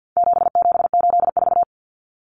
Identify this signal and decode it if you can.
Morse code, heard as a keyed tone, about 50 wpm, 700 Hz. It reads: 7784
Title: Common Cut Numbers - 4-Digit 50wpm